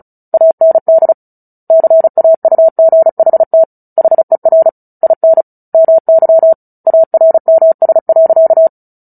Morse code, heard as a keyed tone, about 34 wpm, 650 Hz. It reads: AND CAUGHT HER IN MY ARMS.
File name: Book Two / 09-Chapter 9 - 35 WPM